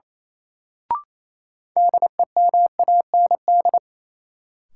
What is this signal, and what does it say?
Morse code, heard as a keyed tone, about 28 wpm, 700 Hz. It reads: DEMAND
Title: Sentences from Top 800 Words - New Word Drill - 28wpm